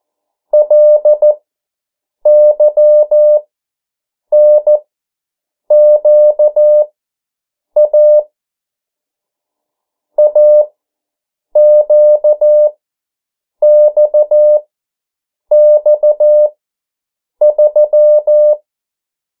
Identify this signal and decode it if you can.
Morse code, heard as a keyed tone, about 14 wpm, spaced as for 7 wpm, 600 Hz. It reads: LYNQA AQXX3
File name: morse mp3 / 09 urok 8